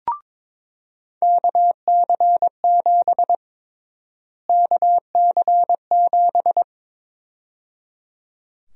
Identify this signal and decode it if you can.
Morse code, heard as a keyed tone, about 22 wpm, 700 Hz. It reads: KC7 KC7
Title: USA Call Sign Prefixes - Repeated 2x - 22wpm